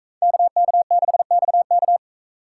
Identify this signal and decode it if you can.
Morse code, heard as a keyed tone, about 42 wpm, 700 Hz. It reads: KK6XK